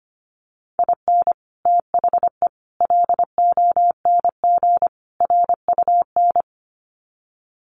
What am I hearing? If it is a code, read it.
Morse code, heard as a keyed tone, about 25 wpm, 700 Hz. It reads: IN THE LONG RUN
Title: Sets of 4 Words 25wpm